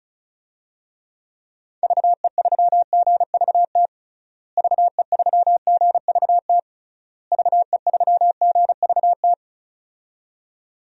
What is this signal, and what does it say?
Morse code, heard as a keyed tone, about 36 wpm, 700 Hz. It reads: VE3GVT VE3GVT VE3GVT
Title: DXCC-100 Call Signs - 6 Characters - Repeated 3x - 35wpm